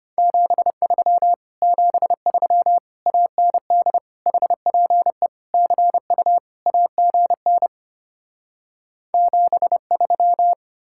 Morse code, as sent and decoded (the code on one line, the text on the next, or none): --... ...-- / --... ...-- / .- -. -.. / .... .--. / -.-. ..- / .- --. -. / --... ...--
73 73 AND HPE CU AGN 73